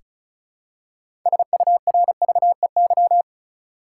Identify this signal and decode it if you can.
Morse code, heard as a keyed tone, about 36 wpm, 700 Hz. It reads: SURVEY
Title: Six Letter Words - 35wpm